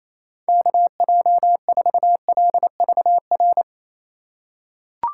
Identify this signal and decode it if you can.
Morse code, heard as a keyed tone, about 28 wpm, 700 Hz. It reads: KJ4LVR
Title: USA Call Signs - 6 Characters - 3x Character Spacing - 28wpm